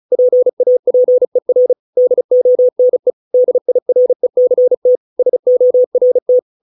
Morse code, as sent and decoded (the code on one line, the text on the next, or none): .--. .- .--. .-. / -.. --- -. / -.. .. .-. -.-. - / ... --- .-. -
PAPER DONE DIRECT SORT